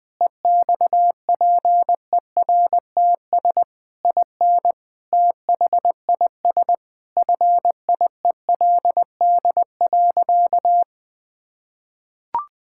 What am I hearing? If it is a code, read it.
Morse code, heard as a keyed tone, about 21 wpm, 700 Hz. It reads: EXPERTS IN THIS FIELD.
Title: Sentences with 5-Word Phrases - 20wpm